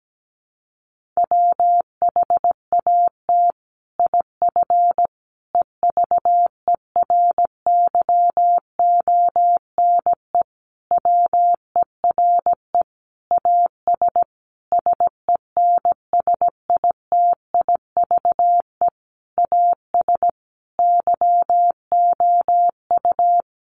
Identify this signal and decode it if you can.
Morse code, heard as a keyed tone, about 17 wpm, 700 Hz. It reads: WHAT IF EVERYONE WERE AS SENSITIVE AS YOU
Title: Sentences from Top 1400 Words 17wpm